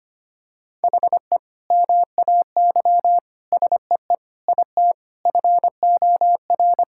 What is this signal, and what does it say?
Morse code, heard as a keyed tone, about 25 wpm, 700 Hz. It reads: HE MAY SEE IT FOR
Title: Sentences from Top 1600 Words 25wpm